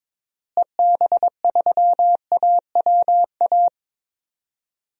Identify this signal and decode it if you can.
Morse code, heard as a keyed tone, about 22 wpm, 700 Hz. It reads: EB3AWA
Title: DXCC-100 Call Signs - 6 Characters - Repeated 2x - 22wpm